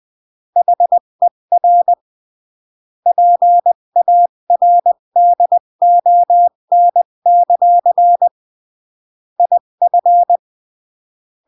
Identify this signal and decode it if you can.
Morse code, heard as a keyed tone, about 20 wpm, 700 Hz. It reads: HER PARDON; IF
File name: CW / jodorowsky / 82 commandments 20-15